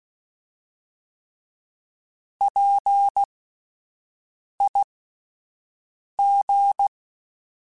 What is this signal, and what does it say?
Morse code, heard as a keyed tone, about 16 wpm, spaced as for 5 wpm, 800 Hz. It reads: PIG